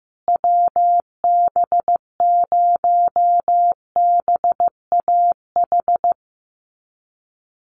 Morse code, heard as a keyed tone, about 15 wpm, 700 Hz. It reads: WB0BAH
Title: USA Call Signs Repeated 3x 15wpm